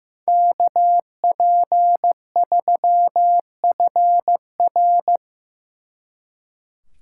Morse code, heard as a keyed tone, about 17 wpm, 700 Hz. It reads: KP3FR
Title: Speed Racing - USA 2x2 Call Signs 15wpm